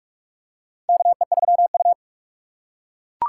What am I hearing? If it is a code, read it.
Morse code, heard as a keyed tone, about 47 wpm, 700 Hz. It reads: KE3U
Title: USA Call Signs - 4 Characters - 6x Character Spacing - 45wpm